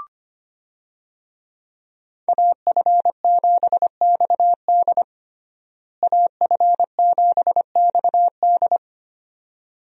Morse code, heard as a keyed tone, about 25 wpm, 700 Hz. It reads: AF7XD AF7XD
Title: USA Call Signs - 5 Characters - Repeated 2x - 25wpm